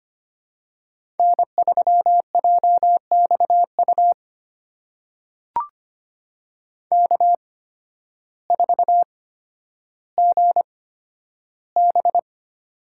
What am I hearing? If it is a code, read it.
Morse code, heard as a keyed tone, about 25 wpm, spaced as for 12 wpm, 700 Hz. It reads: N3JXU K4GB
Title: USA Call Signs - 8x Character Spacing - 25wpm